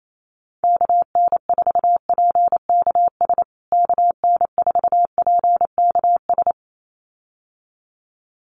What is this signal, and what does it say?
Morse code, heard as a keyed tone, about 28 wpm, 700 Hz. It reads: KN4PKS KN4PKS
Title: USA Call Signs Repeated 2x 28wpm